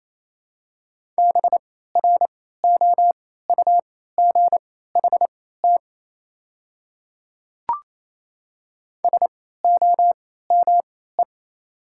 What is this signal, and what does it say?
Morse code, heard as a keyed tone, about 28 wpm, spaced as for 16 wpm, 700 Hz. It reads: BROUGHT SOME
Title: 3x Character Spacing - Top 500 Words - 28wpm